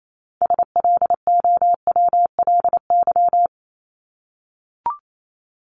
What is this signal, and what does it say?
Morse code, heard as a keyed tone, about 28 wpm, 700 Hz. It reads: SLOWLY
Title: Top 1001-1250 Words 28wpm